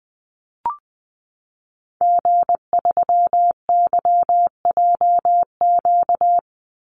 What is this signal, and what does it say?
Morse code, heard as a keyed tone, about 20 wpm, 700 Hz. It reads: G3YJQ
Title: Call Signs Repeated 3x - 20wpm